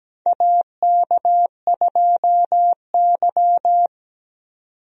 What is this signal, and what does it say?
Morse code, heard as a keyed tone, about 17 wpm, 700 Hz. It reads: AK2Y
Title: USA Call Signs - 4 Characters - 17wpm